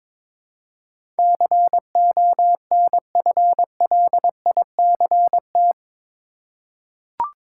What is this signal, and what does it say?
Morse code, heard as a keyed tone, about 22 wpm, 700 Hz. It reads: CONFLICT
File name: practice / 22wpm / Eight Letter Words - 22wpm